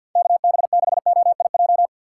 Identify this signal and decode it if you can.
Morse code, heard as a keyed tone, about 53 wpm, 700 Hz. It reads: K B 6 O I J
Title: USA Call Signs - 6 Characters - 6x Character Spacing - 50wpm